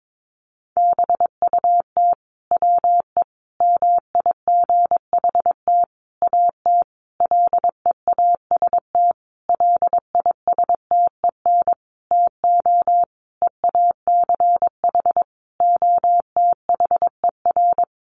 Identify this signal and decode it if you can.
Morse code, heard as a keyed tone, about 22 wpm, 700 Hz. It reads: BUT WE MIGHT AT LEAST LISTEN TO EACH OTHER
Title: Sentences from Top 1300 Words 22wpm